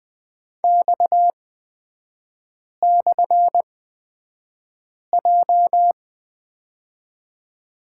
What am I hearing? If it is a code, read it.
Morse code, heard as a keyed tone, about 20 wpm, spaced as for 5 wpm, 700 Hz. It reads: X/J
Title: 255 Lesson 41e - Period - Three Characters 20wpm